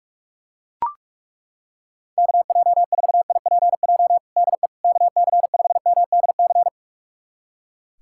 Morse code, heard as a keyed tone, about 42 wpm, 700 Hz. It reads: KJ4IPJ DE KC5MDC
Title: USA Callsign DE Callsign - 6 Characters - 45wpm